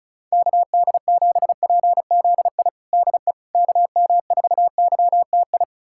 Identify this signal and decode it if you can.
Morse code, heard as a keyed tone, about 34 wpm, 700 Hz. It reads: KD7PZI DE KM4YTI
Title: USA Callsign DE Callsign - 6 Characters - 35wpm